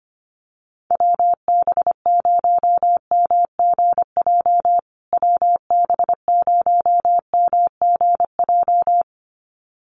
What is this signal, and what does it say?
Morse code, heard as a keyed tone, about 25 wpm, 700 Hz. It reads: WB0MGJ WB0MGJ
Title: USA Call Signs Repeated 2x 25wpm